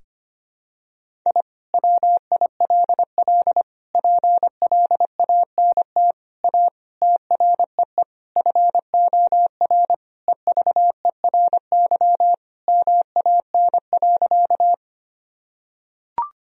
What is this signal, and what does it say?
Morse code, heard as a keyed tone, about 25 wpm, 700 Hz. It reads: I WILL PLANT A TREE FOR EVERY MAN.
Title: Sentences from Top 1000 Words - Review Part 1 - 25wpm